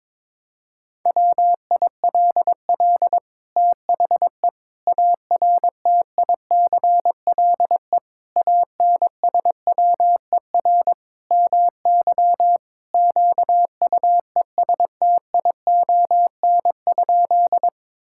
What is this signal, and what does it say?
Morse code, heard as a keyed tone, about 22 wpm, 700 Hz. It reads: WILL THE ARTICLE ANSWER MY QUESTION?
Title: Sentences from Top 1000 Words - Review Part 5 - 22wpm